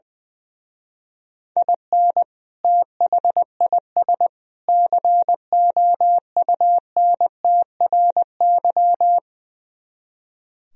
Morse code, heard as a keyed tone, about 21 wpm, 700 Hz. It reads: IN THIS COUNTRY
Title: Speed-Racing - Sets of 3 Words 20wpm